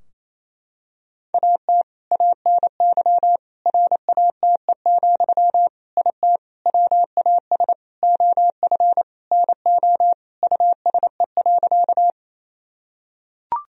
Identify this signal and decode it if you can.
Morse code, heard as a keyed tone, about 28 wpm, 700 Hz. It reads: AT ANY RATE, IT WAS OF NO USE.
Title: Sentences from Top 1000 Words - Review Part 4 - 28wpm